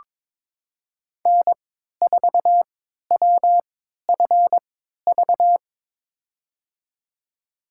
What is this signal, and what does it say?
Morse code, heard as a keyed tone, about 22 wpm, spaced as for 13 wpm, 700 Hz. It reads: N4WFV
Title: USA Call Signs - 3x Character Spacing - 22wpm